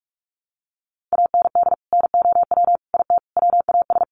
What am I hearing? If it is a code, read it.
Morse code, heard as a keyed tone, about 44 wpm, 700 Hz. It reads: AND NOW IT WAS
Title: Sentences from Top 1100 Words 45wpm